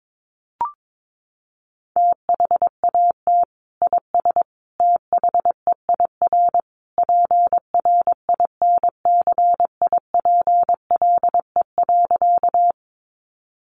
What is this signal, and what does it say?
Morse code, heard as a keyed tone, about 22 wpm, 700 Hz. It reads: THAT IS THEIR PRINCIPLE.
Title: Sentences from Top 1400 Words 22wpm